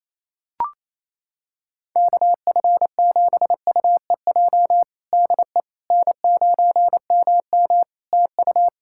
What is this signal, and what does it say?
Morse code, heard as a keyed tone, about 28 wpm, 700 Hz. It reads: KF7UEJ DE N9MM TU
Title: Practice QSO - Exchange 4 - Ending - 28wpm